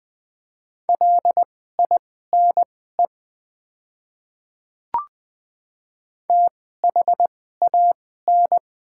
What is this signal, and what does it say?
Morse code, heard as a keyed tone, about 20 wpm, spaced as for 15 wpm, 700 Hz. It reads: LINE THAN
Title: 2x Character Spacing - Top 200 Words - 20wpm